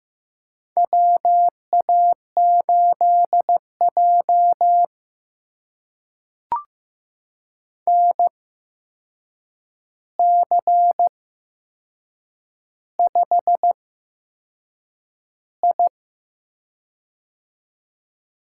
Morse code, heard as a keyed tone, about 15 wpm, spaced as for 6 wpm, 700 Hz. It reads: WA8J NC5I
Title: USA Call Signs - 4 Characters - 8x Character Spacing - 15wpm